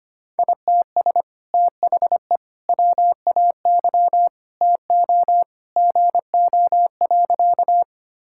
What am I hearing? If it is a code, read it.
Morse code, heard as a keyed tone, about 25 wpm, 700 Hz. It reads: ITS THE WAY TO GO.
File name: practice / 25wpm / Sentences with 2-Word Phrases - 25wpm